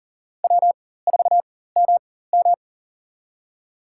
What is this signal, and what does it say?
Morse code, heard as a keyed tone, about 37 wpm, spaced as for 18 wpm, 700 Hz. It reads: W4MM
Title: USA Call Signs - 4 Characters - 4x Character Spacing - 40wpm